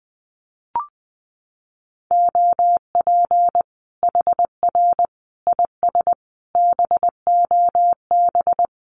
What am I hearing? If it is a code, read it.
Morse code, heard as a keyed tone, about 20 wpm, 700 Hz. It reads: OP HR IS BOB
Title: Common QSO Phrases 20wpm